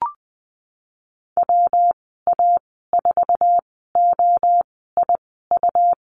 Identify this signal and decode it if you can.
Morse code, heard as a keyed tone, about 20 wpm, spaced as for 15 wpm, 700 Hz. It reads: WA4OIU
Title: 147 Lesson 24g - Letter y - Call Signs 20wpm